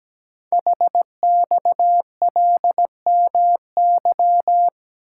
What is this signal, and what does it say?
Morse code, heard as a keyed tone, about 17 wpm, 700 Hz. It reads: HXLMY
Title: Code Groups - 5 Letters - 17wpm